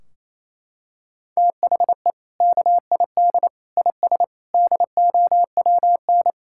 Morse code, heard as a keyed tone, about 28 wpm, 700 Hz. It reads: THE KID IS DOWN
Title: Sentences from Top 1600 Words 28wpm